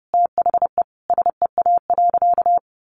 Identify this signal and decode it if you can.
Morse code, heard as a keyed tone, about 28 wpm, 700 Hz. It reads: THE SEA.
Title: Sentences from Top 1300 Words 30wpm